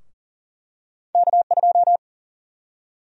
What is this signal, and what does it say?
Morse code, heard as a keyed tone, about 40 wpm, 700 Hz. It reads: K2
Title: USA Call Sign Prefixes - Repeated 2x - 40wpm